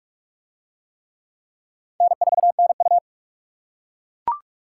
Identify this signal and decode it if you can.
Morse code, heard as a keyed tone, about 41 wpm, 700 Hz. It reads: N4NU